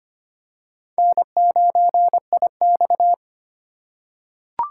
Morse code, heard as a keyed tone, about 25 wpm, 700 Hz. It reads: N9IX
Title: USA Call Signs - 2x Character Spacing - 25wpm